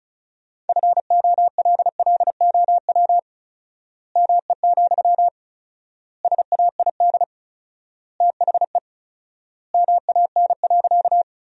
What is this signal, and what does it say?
Morse code, heard as a keyed tone, about 34 wpm, 700 Hz. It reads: FOLLOW ME, SAID THE MAN.
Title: 4x Word-Spacing - Sentences from Top 200 Words - 35wpm